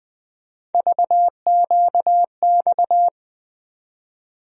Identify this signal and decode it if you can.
Morse code, heard as a keyed tone, about 21 wpm, 700 Hz. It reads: VQX
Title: USA Call Sign Postfixes - 3 Letters 20wpm